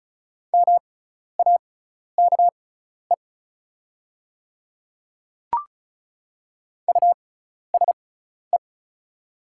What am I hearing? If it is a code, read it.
Morse code, heard as a keyed tone, about 36 wpm, spaced as for 12 wpm, 700 Hz. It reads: MAKE USE
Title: 6x Character Spacing - Top 100 Words - 35wpm